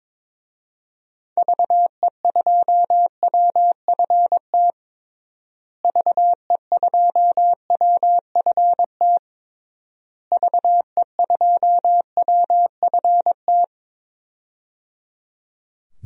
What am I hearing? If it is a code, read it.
Morse code, heard as a keyed tone, about 22 wpm, 700 Hz. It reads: VE2WFT VE2WFT VE2WFT
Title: DXCC-100 Call Signs - 6 Characters - Repeated 3x - 22wpm